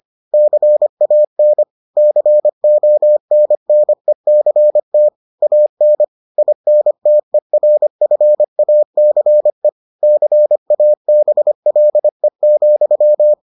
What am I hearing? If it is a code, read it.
Morse code, heard as a keyed tone, about 25 wpm, 600 Hz. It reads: CAN CONNECT AN INTERFACE CABLE,